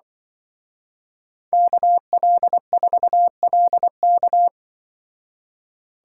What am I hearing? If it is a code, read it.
Morse code, heard as a keyed tone, about 23 wpm, 700 Hz. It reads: KL4LK
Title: Speed Racing - USA Call Signs 20wpm